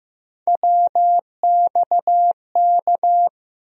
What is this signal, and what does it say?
Morse code, heard as a keyed tone, about 15 wpm, 700 Hz. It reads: WXK
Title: USA Call Sign Postfixes - 3 Letters 15wpm